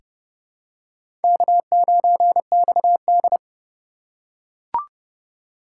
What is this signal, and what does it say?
Morse code, heard as a keyed tone, about 30 wpm, 700 Hz. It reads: K9XD
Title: K1USN SST QSO Practice - 30wpm